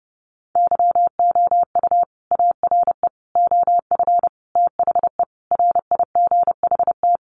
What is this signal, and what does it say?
Morse code, heard as a keyed tone, about 30 wpm, 700 Hz. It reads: YOU ARE OF THE RIGHT
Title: Sentences from Top 1400 Words 30wpm